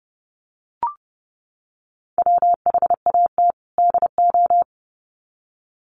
Morse code, heard as a keyed tone, about 30 wpm, 700 Hz. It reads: WHAT DO
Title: Sets of 2 Words 30wpm